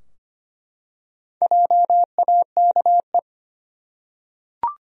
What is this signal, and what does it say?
Morse code, heard as a keyed tone, about 25 wpm, 700 Hz. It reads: JAKE
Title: USA Common Names - 25wpm